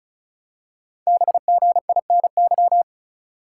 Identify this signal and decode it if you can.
Morse code, heard as a keyed tone, about 36 wpm, 700 Hz. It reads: BGINY
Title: Code Groups - 5 Letters - 35wpm